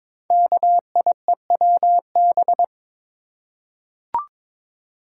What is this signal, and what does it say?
Morse code, heard as a keyed tone, about 23 wpm, 700 Hz. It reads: KIEWB
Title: Code Groups - 5 Letters - 22wpm